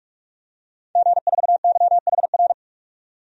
Morse code, heard as a keyed tone, about 46 wpm, 700 Hz. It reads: G4YHR
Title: DXCC-100 Call Signs - 5 Characters - 45wpm